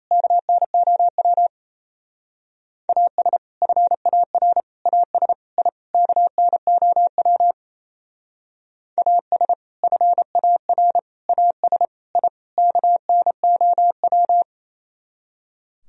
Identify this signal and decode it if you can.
Morse code, heard as a keyed tone, about 36 wpm, 700 Hz. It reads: KNOW AS FAR AS I KNOW AS FAR AS I KNOW